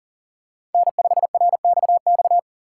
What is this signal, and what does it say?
Morse code, heard as a keyed tone, about 40 wpm, 700 Hz. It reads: N5RXX
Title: USA Call Signs - 5 Characters - 2x Character Spacing - 40wpm